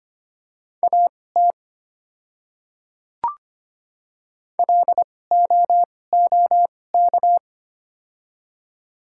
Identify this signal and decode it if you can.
Morse code, heard as a keyed tone, about 25 wpm, spaced as for 18 wpm, 690 Hz. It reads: AT LOOK
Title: 2x Character Spacing - Top 200 Words - 25wpm